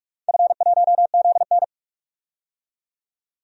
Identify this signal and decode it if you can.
Morse code, heard as a keyed tone, about 46 wpm, 700 Hz. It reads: F1ZN
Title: DXCC-100 Call Signs - 4 Characters - 45wpm